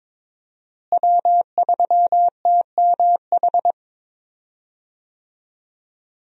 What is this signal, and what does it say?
Morse code, heard as a keyed tone, about 22 wpm, 700 Hz. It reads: W3TMH
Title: USA Call Signs - 5 Characters - Repeated 2x - 22wpm